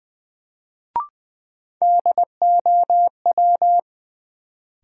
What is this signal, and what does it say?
Morse code, heard as a keyed tone, about 20 wpm, 700 Hz. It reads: DOW
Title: Sets of 5 Words - New Word Drill - 20wpm